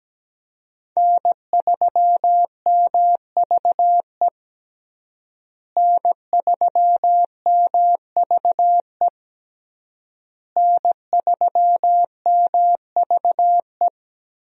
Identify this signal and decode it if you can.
Morse code, heard as a keyed tone, about 17 wpm, 700 Hz. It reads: N3MVE N3MVE N3MVE
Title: USA Call Signs - 5 Characters - Repeated 3x - 17wpm